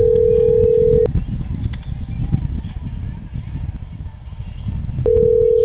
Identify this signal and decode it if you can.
Ringback tone (North America), 440 + 480 Hz.